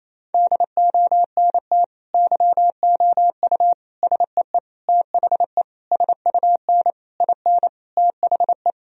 Morse code, heard as a keyed tone, about 28 wpm, 700 Hz. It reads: DONT YOU SEE THE SUN IN THE
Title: Sentences from Top 1000 Words - Review Part 2 - 28wpm